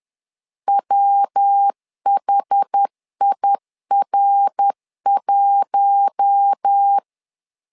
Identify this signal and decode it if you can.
Morse code, heard as a keyed tone, about 10 wpm, 800 Hz. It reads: WHIR1
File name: LEZIONI / 4 Q1S 10wpm 2133